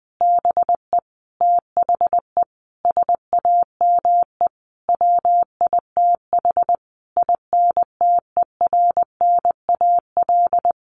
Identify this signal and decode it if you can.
Morse code, heard as a keyed tone, about 20 wpm, 700 Hz. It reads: BE THE SAME WITH INTERNAL
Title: Sentences from Top 1400 Words 20wpm